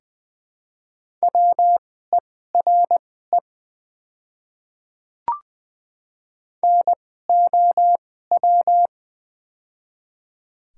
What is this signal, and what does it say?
Morse code, heard as a keyed tone, about 20 wpm, spaced as for 14 wpm, 700 Hz. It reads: WERE NOW
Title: 2x Character Spacing - Top 100 Words - 20wpm